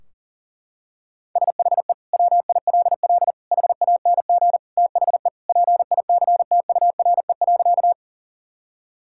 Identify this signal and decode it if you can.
Morse code, heard as a keyed tone, about 37 wpm, 700 Hz. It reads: SHE WILL HANG THE PICTURE.
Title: Sentences from Top 1600 Words 40wpm